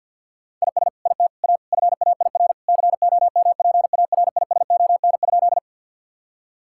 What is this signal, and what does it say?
Morse code, heard as a keyed tone, about 50 wpm, 700 Hz. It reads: IS IT A FAIR COMPARISON?
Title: Sentences from Top 1600 Words 50wpm